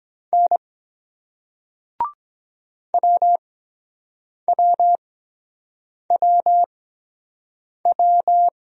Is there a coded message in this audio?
Yes: Morse code from 0.3 to 8.5 s, heard as a keyed tone, about 23 wpm, 700 Hz. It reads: N W W W W